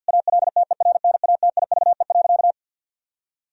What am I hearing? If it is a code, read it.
Morse code, heard as a keyed tone, about 50 wpm, 700 Hz. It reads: ALTERNATIVE.
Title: Sentences from Top 1000 Words - Review Part 4 - 50wpm